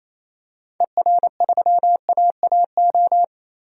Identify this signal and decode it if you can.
Morse code, heard as a keyed tone, about 28 wpm, 700 Hz. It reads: ER3AAO
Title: DXCC-100 Call Signs - 6 Characters - 28wpm